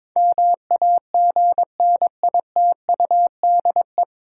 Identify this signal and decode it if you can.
Morse code, heard as a keyed tone, about 22 wpm, 700 Hz. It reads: MAGNITUDE
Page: Morse Code Ninja practice 22wpm